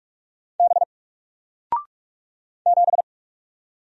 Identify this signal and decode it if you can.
Morse code, heard as a keyed tone, about 45 wpm, 700 Hz. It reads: B 7
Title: Single Letter-Number 45wpm